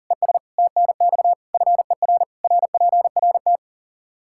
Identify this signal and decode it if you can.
Morse code, heard as a keyed tone, about 37 wpm, 700 Hz. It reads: ES TNX FER RPRT
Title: Practice QSO - Exchange 2 - Info - 40wpm